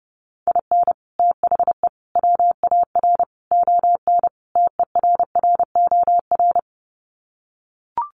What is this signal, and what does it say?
Morse code, heard as a keyed tone, about 28 wpm, 700 Hz. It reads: IN THE WAR ON TERROR
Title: Sets of 5 Words Encore2 30wpm